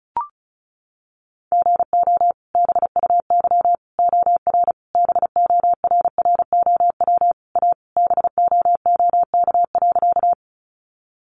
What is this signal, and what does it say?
Morse code, heard as a keyed tone, about 34 wpm, 700 Hz. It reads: GO BUY OR BORROW A BOOK.
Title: Sentences from Top 1500 Words 35wpm